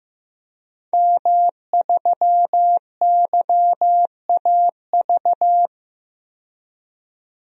Morse code, heard as a keyed tone, about 15 wpm, 700 Hz. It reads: M3YAV
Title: DXCC-100 Call Signs - 5 Characters - 15wpm